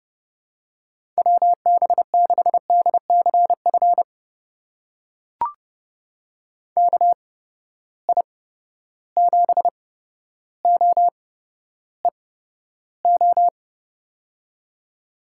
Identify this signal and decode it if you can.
Morse code, heard as a keyed tone, about 30 wpm, spaced as for 13 wpm, 700 Hz. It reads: WB6DCF KI7OEO
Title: USA Call Signs - 6 Characters - 8x Character Spacing - 30wpm